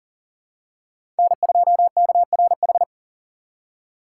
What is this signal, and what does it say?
Morse code, heard as a keyed tone, about 40 wpm, 700 Hz. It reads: N2KRH
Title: USA Call Signs - 4x Character Spacing - 40wpm